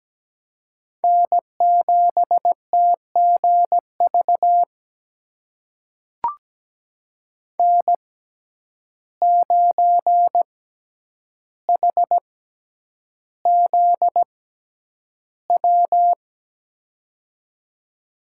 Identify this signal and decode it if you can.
Morse code, heard as a keyed tone, about 17 wpm, spaced as for 9 wpm, 700 Hz. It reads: N7TGV N9HZW
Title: USA Call Signs - 5 Characters - 6x Character Spacing - 17wpm